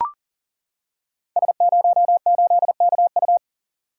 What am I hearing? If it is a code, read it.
Morse code, heard as a keyed tone, about 37 wpm, 700 Hz. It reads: S08KU
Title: Code Groups - 5 Letter:Numbers - 40wpm